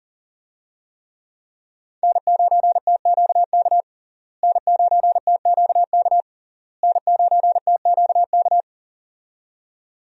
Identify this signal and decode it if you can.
Morse code, heard as a keyed tone, about 37 wpm, 700 Hz. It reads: N9TQK N9TQK N9TQK
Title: USA Call Signs - 5 Characters - Repeated 3x - 40wpm